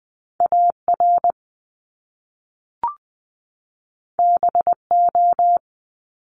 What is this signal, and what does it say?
Morse code, heard as a keyed tone, about 20 wpm, 700 Hz. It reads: AR BO
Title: Top Two Letter Combinations in Words 20wpm